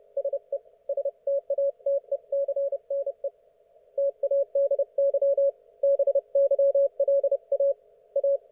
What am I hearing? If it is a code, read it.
Morse code, heard as a keyed tone, about 30 wpm, 560 Hz. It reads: SE STATECNE TADY BYLA A
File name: Archiv / CUC179 / ok1fhi cul179